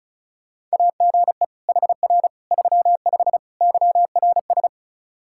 Morse code, heard as a keyed tone, about 36 wpm, 700 Hz. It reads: AGE HR 35 YRS
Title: Practice QSO - Exchange 3 - More Info - 35wpm